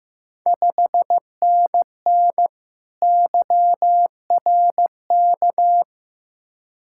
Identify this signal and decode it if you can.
Morse code, heard as a keyed tone, about 15 wpm, 700 Hz. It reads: 5NN YRK
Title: Ontario QSO Party Exchange 15wpm